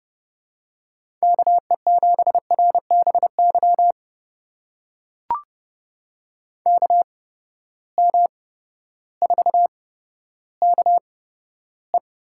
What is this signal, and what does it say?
Morse code, heard as a keyed tone, about 30 wpm, spaced as for 14 wpm, 700 Hz. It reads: KE7RBY KM4KE
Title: USA Call Signs - 6 Characters - 8x Character Spacing - 30wpm